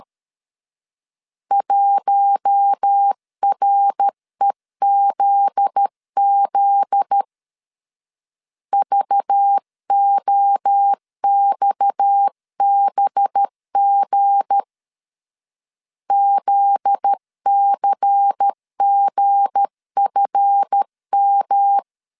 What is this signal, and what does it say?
Morse code, heard as a keyed tone, about 13 wpm, 800 Hz. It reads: 1REZZ VOXBG ZCGFM
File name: LEZIONI / 7 GF9Z / 7 GF9Z 12wpm 2147